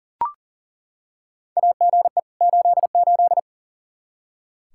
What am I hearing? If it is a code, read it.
Morse code, heard as a keyed tone, about 40 wpm, 700 Hz. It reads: AGE 88
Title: Practice QSO - Exchange 3 - More Info - 40wpm